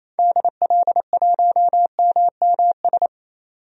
Morse code, heard as a keyed tone, about 28 wpm, 700 Hz. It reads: DL1MMS
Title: DXCC-100 Call Signs - 6 Characters - 28wpm